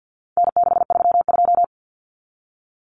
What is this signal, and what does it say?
Morse code, heard as a keyed tone, about 50 wpm, 700 Hz. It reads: N632
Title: Common Cut Numbers - 4-Digit 50wpm